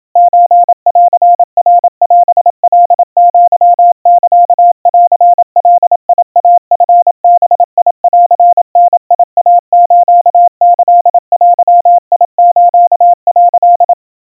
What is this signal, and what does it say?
Morse code (Japanese Wabun code), heard as a keyed tone, about 27 wpm, 700 Hz. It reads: ソンナオカアサンガイチバンダイスキデス」
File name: kawa / Morse sample / kamyu 100 cpm 700 Hz Crispy